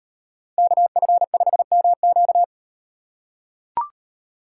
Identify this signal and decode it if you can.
Morse code, heard as a keyed tone, about 38 wpm, 700 Hz. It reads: KF5MQ